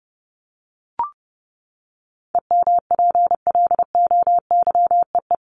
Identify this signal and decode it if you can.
Morse code, heard as a keyed tone, about 30 wpm, 700 Hz. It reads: EMPLOYEE
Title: Top 1501-1750 Words 30wpm